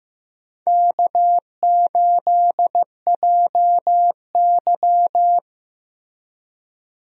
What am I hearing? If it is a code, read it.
Morse code, heard as a keyed tone, about 15 wpm, 700 Hz. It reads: K8JY
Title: USA Call Signs - 4 Characters - Repeated 2x - 15wpm